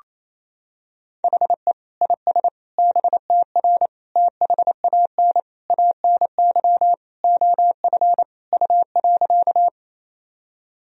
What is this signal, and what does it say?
Morse code, heard as a keyed tone, about 28 wpm, 700 Hz. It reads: HE IS BTR THAN ANY OF U.
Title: Sentences from Top 400 Words and Abbreviations 28wpm